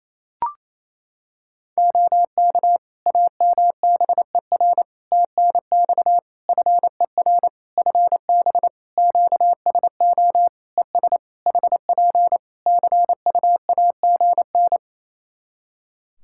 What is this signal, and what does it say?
Morse code, heard as a keyed tone, about 27 wpm, 700 Hz. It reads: OK AMBER TNX FER FB QSO ES HP CUAGN
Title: Practice QSO - Exchange 4 - Ending - 28wpm